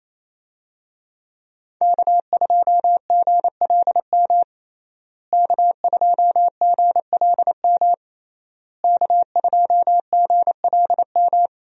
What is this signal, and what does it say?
Morse code, heard as a keyed tone, about 28 wpm, 700 Hz. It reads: K2GLM K2GLM K2GLM
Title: USA Call Signs - 5 Characters - Repeated 3x - 28wpm